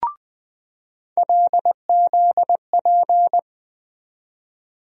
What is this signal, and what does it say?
Morse code, heard as a keyed tone, about 20 wpm, 700 Hz. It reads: LZP
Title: USA Call Sign Postfixes - 3 Letters 20wpm